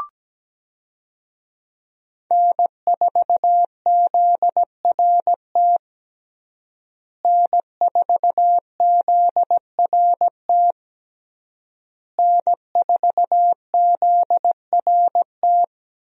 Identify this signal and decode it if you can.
Morse code, heard as a keyed tone, about 17 wpm, 700 Hz. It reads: N4ZRT N4ZRT N4ZRT
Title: USA Call Signs - 5 Characters - Repeated 3x - 17wpm